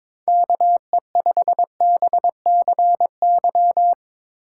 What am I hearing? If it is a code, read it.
Morse code, heard as a keyed tone, about 22 wpm, 700 Hz. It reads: KE5BCY
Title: K1USN SST QSO Practice - 22wpm